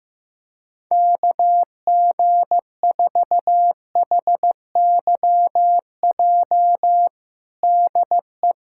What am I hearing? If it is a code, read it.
Morse code, heard as a keyed tone, about 15 wpm, 700 Hz. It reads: KG4HYJ DE
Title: USA Callsign DE Callsign - 6 Characters - 15wpm